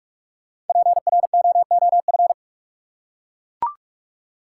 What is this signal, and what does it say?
Morse code, heard as a keyed tone, about 43 wpm, 700 Hz. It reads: PROOF
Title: Five Letter Words - 45wpm